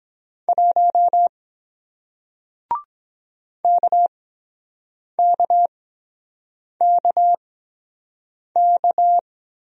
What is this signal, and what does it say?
Morse code, heard as a keyed tone, about 23 wpm, spaced as for 7 wpm, 700 Hz. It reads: 1 KKKK